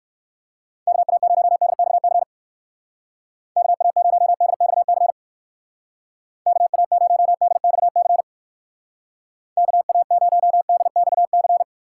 Morse code, heard as a keyed tone, about 59 wpm, 700 Hz. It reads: KA0DXC KA0DXC KA0DXC KA0DXC